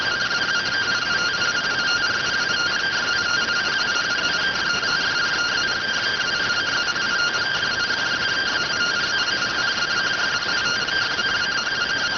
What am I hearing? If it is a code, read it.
Two-tone teleprinter signal, about 170 Hz shift.